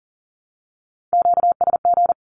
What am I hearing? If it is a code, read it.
Morse code, heard as a keyed tone, about 40 wpm, 700 Hz. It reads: QSG
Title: Q-Codes Repeated 3x 40wpm